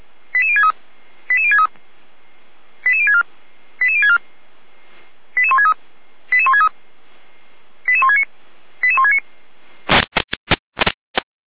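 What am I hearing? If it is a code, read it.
Five-tone selective call (ZVEI), heard as quick runs of five tones, tones of about 70 ms. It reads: 88063 88063 88064 88064 88163 88163 88169 88169
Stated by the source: ZVEI five-tone selective call received by SDR.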